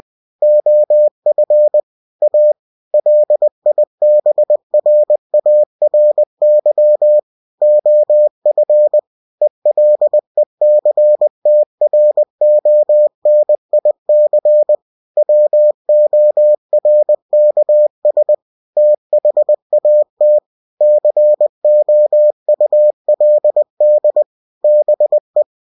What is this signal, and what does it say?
Morse code, heard as a keyed tone, about 20 wpm, 600 Hz. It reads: OF A LIBRARY OF ELECTRONIC WORKS THAT COULD BE
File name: cw-books / winnie-the-pooh / winnie-20wpm-page-0122